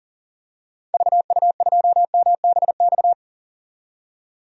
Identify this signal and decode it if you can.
Morse code, heard as a keyed tone, about 40 wpm, 700 Hz. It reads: VU2MBX
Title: DXCC-100 Call Signs - 6 Characters - 40wpm